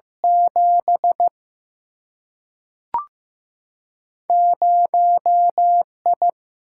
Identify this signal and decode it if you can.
Morse code, heard as a keyed tone, about 15 wpm, 700 Hz. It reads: 7 0I
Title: Single-Letter Single-Number 15wpm